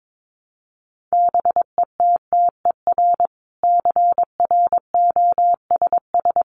Morse code, heard as a keyed tone, about 22 wpm, 700 Hz. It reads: BETTER CROSS
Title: Sentences from Top 400 Words 22wpm